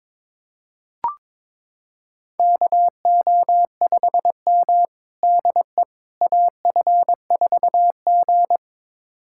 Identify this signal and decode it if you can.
Morse code, heard as a keyed tone, about 22 wpm, 700 Hz. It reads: KO5M DE AF4G
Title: USA Callsign DE Callsign - 4 Characters - 22wpm